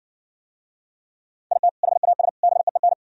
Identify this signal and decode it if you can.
Morse code, heard as a keyed tone, about 65 wpm, 700 Hz. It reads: IT HAS BEEN